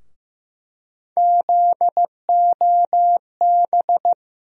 Morse code, heard as a keyed tone, about 15 wpm, 700 Hz. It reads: ZOB
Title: USA Call Sign Postfixes - 3 Letters 15wpm